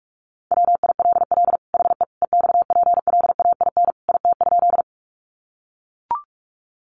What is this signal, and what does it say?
Morse code, heard as a keyed tone, about 44 wpm, 700 Hz. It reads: WILL HE EXPLAIN IT?
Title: Sentences from Top 1100 Words 45wpm